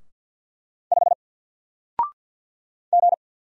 Morse code, heard as a keyed tone, about 48 wpm, 700 Hz. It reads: 5 G
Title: Single Character (Letter:Number:Punctuation:Prosign) Alt 50wpm